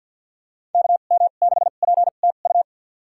Morse code, heard as a keyed tone, about 51 wpm, spaced as for 36 wpm, 700 Hz. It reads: KM6PTU
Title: USA Call Signs - 2x Character Spacing - 50wpm